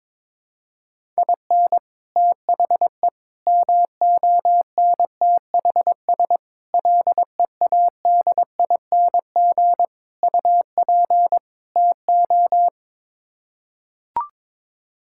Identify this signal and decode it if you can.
Morse code, heard as a keyed tone, about 22 wpm, 700 Hz. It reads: IN THE MONTHS LEADING UP TO
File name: practice / 22wpm / Sets of 6 Words - 22wpm